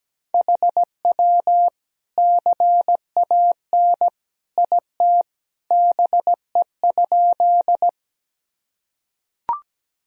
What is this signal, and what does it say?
Morse code, heard as a keyed tone, about 17 wpm, 700 Hz. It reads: HW CAN IT BE?
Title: Sentences from Top 100 Words and Abbreviations 17wpm